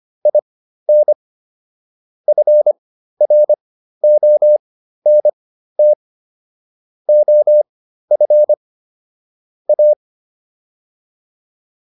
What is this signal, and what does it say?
Morse code, heard as a keyed tone, about 25 wpm, spaced as for 13 wpm, 600 Hz. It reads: IN FRONT OF A